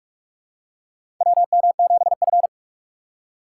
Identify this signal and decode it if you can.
Morse code, heard as a keyed tone, about 44 wpm, 700 Hz. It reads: WM7F
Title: USA Call Signs - 4 Characters - 8x Character Spacing - 45wpm